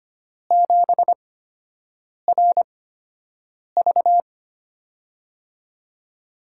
Morse code, heard as a keyed tone, about 25 wpm, spaced as for 7 wpm, 700 Hz. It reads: 7RV
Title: USA Call Signs - 5 Characters - 8x Character Spacing - 25wpm